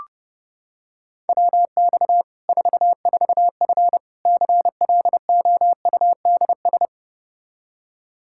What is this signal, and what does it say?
Morse code, heard as a keyed tone, about 32 wpm, 700 Hz. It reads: WX 44F CLOUDS
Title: K1USN SST QSO Practice 30wpm